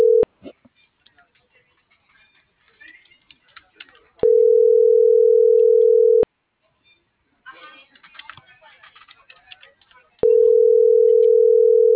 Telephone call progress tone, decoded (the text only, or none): ringback tone